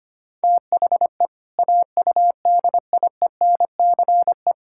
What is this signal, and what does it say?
Morse code, heard as a keyed tone, about 25 wpm, 700 Hz. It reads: THE AUDIENCE
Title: Sentences from Top 1000 Words - Review Part 6 - 25wpm